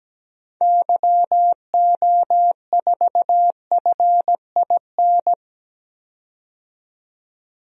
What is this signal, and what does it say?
Morse code, heard as a keyed tone, about 17 wpm, 700 Hz. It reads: YO4FIN
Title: DXCC-100 Call Signs - 6 Characters - 17wpm